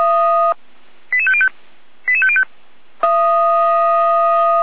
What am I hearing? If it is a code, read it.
Five-tone selective call (ZVEI), heard as quick runs of five tones, tones of about 70 ms. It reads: 88505 88505